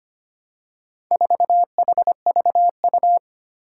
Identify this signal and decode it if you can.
Morse code, heard as a keyed tone, about 25 wpm, 700 Hz. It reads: 4HVU
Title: Four Challenging Letter-Number Combinations 25wpm